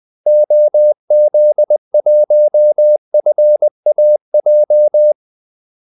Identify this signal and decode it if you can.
Morse code, heard as a keyed tone, about 20 wpm, 600 Hz. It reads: OZ1FAJ
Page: Kaldesignaler 20 wpm